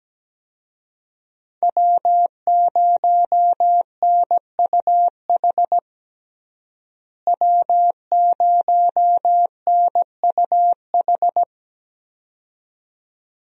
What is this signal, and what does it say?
Morse code, heard as a keyed tone, about 17 wpm, 700 Hz. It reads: W0NUH W0NUH
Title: USA Call Signs - 5 Characters - Repeated 2x - 17wpm